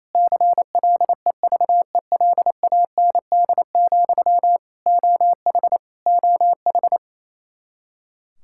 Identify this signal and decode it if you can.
Morse code, heard as a keyed tone, about 27 wpm, 700 Hz. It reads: CLEVELAND, OH OH
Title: K1USN SST QSO Practice 28wpm